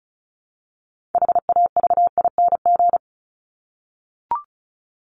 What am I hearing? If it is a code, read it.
Morse code, heard as a keyed tone, about 34 wpm, 700 Hz. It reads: HAVING
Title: ING Suffix - Top 100 Words with Base Words 35wpm